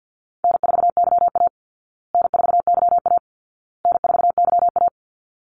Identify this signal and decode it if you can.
Morse code, heard as a keyed tone, about 50 wpm, 700 Hz. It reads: N4YA N4YA N4YA
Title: Call Signs Repeated 2x - 50wpm